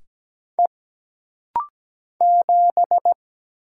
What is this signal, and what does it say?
Morse code, heard as a keyed tone, about 17 wpm, 700 Hz. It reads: E 7
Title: Single Letter-Number Alt - Rapid-Fire 17wpm